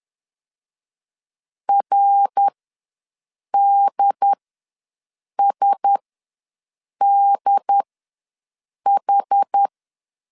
Morse code, heard as a keyed tone, about 10 wpm, spaced as for 6 wpm, 800 Hz. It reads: RDSDH
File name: LEZIONI / 5 LBVC / 5 L ws 10wpm 3636